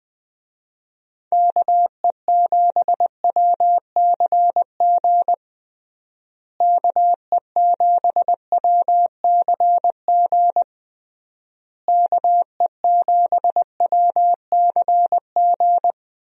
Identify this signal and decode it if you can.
Morse code, heard as a keyed tone, about 20 wpm, 700 Hz. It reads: KE7WCG KE7WCG KE7WCG
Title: USA Call Signs - 6 Characters - Repeated 3x - 20wpm